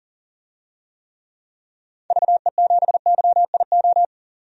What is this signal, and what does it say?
Morse code, heard as a keyed tone, about 40 wpm, 700 Hz. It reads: VE7YIO
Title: DXCC-100 Call Signs - 6 Characters - 40wpm